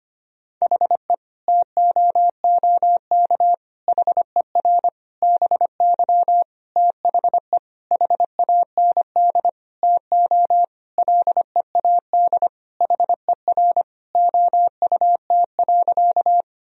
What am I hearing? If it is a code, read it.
Morse code, heard as a keyed tone, about 25 wpm, 700 Hz. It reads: HE TOOK HER BY THE HAND TO LEAD HER OUT.
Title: Sentences from Top 1000 Words - Review Part 2 - 25wpm